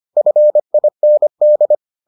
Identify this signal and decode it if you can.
Morse code, heard as a keyed tone, about 25 wpm, 600 Hz. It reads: FIND